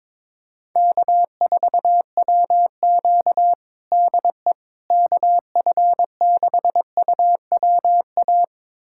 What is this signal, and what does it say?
Morse code, heard as a keyed tone, about 22 wpm, 700 Hz. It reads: K4WQ DE KF6UWA
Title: USA Callsign DE Callsign 22wpm